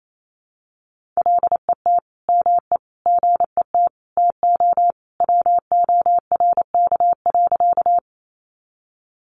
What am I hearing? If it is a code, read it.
Morse code, heard as a keyed tone, about 28 wpm, 700 Hz. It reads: LET ME GET TO WORK.
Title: Sentences from Top 300 Words 28wpm